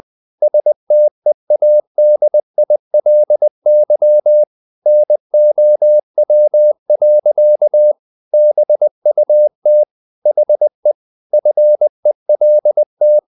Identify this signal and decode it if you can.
Morse code, heard as a keyed tone, about 20 wpm, 600 Hz. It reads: STEADILY NOW. BUT HE FELT
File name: the old man and the sea part 2 20wpm